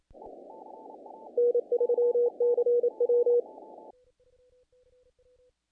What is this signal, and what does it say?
Morse code, heard as a keyed tone, about 28 wpm, 480 Hz. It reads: N3CW
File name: contests / naqp-cw-k5tr-op-k5pi- -1800